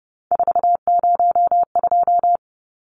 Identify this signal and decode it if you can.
Morse code, heard as a keyed tone, about 30 wpm, 700 Hz. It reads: 402
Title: Numbers - 3-Digit 30wpm